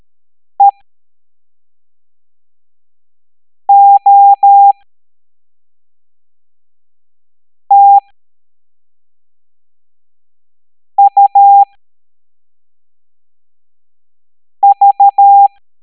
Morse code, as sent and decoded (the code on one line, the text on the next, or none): . --- - ..- ...-
EOTUV